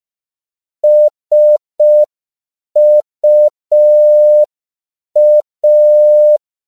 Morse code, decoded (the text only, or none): SUA